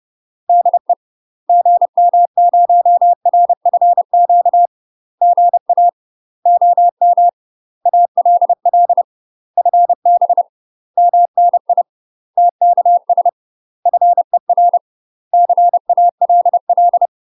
Morse code, heard as a keyed tone, about 30 wpm, 700 Hz. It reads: DE GM0RFQ GA OM ALL FB MNI TKS FER CALL
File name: UK MASTER QSOs TRAINING / 30WPM / UK MASTER QSOs 0015